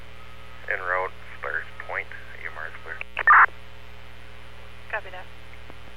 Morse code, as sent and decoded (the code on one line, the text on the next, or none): none